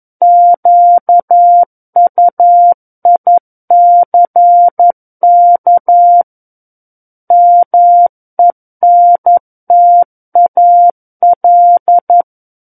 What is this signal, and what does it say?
Morse code, heard as a keyed tone, about 11 wpm, 700 Hz. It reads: QUICK MENTAL
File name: 11 WPM